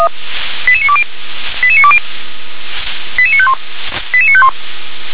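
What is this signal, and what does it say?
Five-tone selective call (ZVEI), heard as quick runs of five tones, tones of about 70 ms. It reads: 88920 88920 88951 88951
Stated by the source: ZVEI five-tone call 88956 received on SDR.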